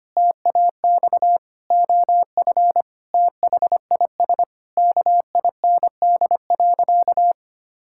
Morse code, heard as a keyed tone, about 25 wpm, 700 Hz. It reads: TAX OF THIS KIND.
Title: Sentences from Top 1000 Words - Review Part 5 - 25wpm